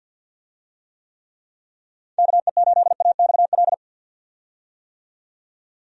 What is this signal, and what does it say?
Morse code, heard as a keyed tone, about 47 wpm, 700 Hz. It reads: KE8AXL
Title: USA Call Signs - 6 Characters - 50wpm